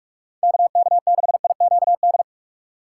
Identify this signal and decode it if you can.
Morse code, heard as a keyed tone, about 48 wpm, 700 Hz. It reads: KK6IQD